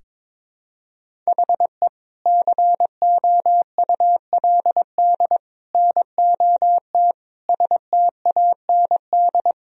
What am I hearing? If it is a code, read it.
Morse code, heard as a keyed tone, about 22 wpm, 700 Hz. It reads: HE COULD NOT STAND
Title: Sentences from Top 1000 Words - Review Part 1 - 22wpm